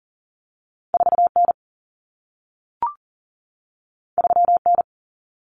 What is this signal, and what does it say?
Morse code, heard as a keyed tone, about 40 wpm, 700 Hz. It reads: 4N 3N
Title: Common Cut Numbers - 2-Digit 40wpm